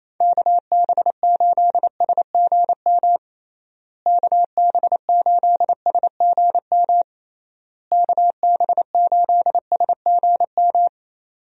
Morse code, heard as a keyed tone, about 28 wpm, 700 Hz. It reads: KB8SGM KB8SGM KB8SGM
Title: USA Call Signs - 6 Characters - Repeated 3x - 28wpm